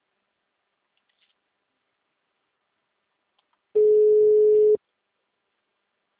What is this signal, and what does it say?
Ringback tone (Europe, 425 Hz).
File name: out- -2010- - - .53458